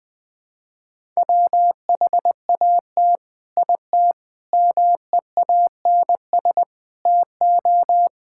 Morse code, heard as a keyed tone, about 19 wpm, 700 Hz. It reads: WHAT IT MEANS TO
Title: Sets of 6 Words - 20wpm